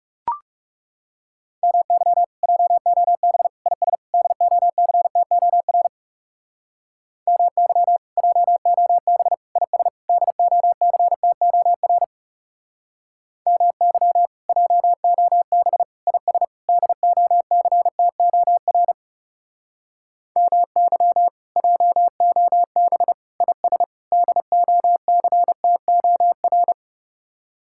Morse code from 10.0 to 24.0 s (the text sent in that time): DOCTOR MY JOB IS DOCTOR MY JOB IS